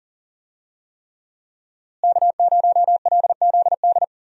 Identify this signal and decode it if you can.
Morse code, heard as a keyed tone, about 40 wpm, 700 Hz. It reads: K0LZD
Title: USA Call Signs - 5 Characters - Repeated 3x - 40wpm